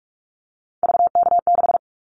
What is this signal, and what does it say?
Morse code, heard as a keyed tone, about 44 wpm, 700 Hz. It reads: VK6
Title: Callsign Prefixes for Top 100 DXCC - 45wpm